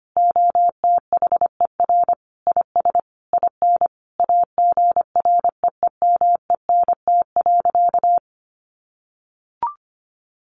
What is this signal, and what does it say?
Morse code, heard as a keyed tone, about 25 wpm, 700 Hz. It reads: OTHER IS IN AGREEMENT.
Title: Sentences from Top 1500 Words 25wpm